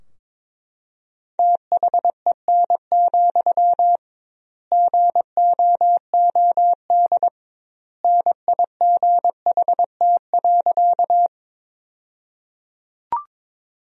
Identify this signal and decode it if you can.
Morse code, heard as a keyed tone, about 22 wpm, 700 Hz. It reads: THEN, GOOD NIGHT.
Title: 2x Word-Spacing - Sentences from Top 300 Words - 22wpm